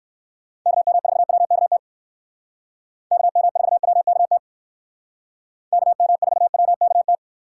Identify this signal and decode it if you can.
Morse code, heard as a keyed tone, about 63 wpm, 700 Hz. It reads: KM4WKT KM4WKT KM4WKT